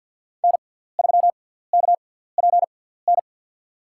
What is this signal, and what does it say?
Morse code, heard as a keyed tone, about 52 wpm, spaced as for 17 wpm, 700 Hz. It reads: N3KPN
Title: USA Call Signs - 6x Character Spacing - 50wpm